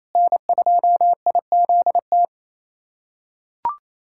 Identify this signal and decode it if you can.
Morse code, heard as a keyed tone, about 28 wpm, 700 Hz. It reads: N2IZT
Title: USA Callsign DE Callsign - 5 Characters - 28wpm